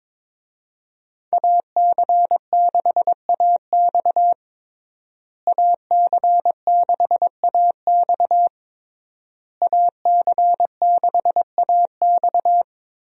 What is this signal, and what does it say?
Morse code, heard as a keyed tone, about 22 wpm, 700 Hz. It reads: AC6AX AC6AX AC6AX
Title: USA Call Signs - 5 Characters - Repeated 3x - 22wpm